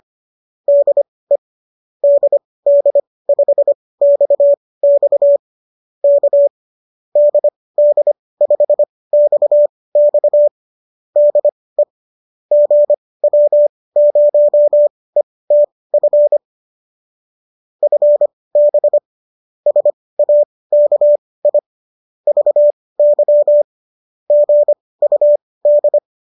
Morse code, heard as a keyed tone, about 25 wpm, spaced as for 18 wpm, 590 Hz. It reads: DE DD5XX K DD5XX DE GW0ETF FB SAKI VY GUD
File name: qso218 18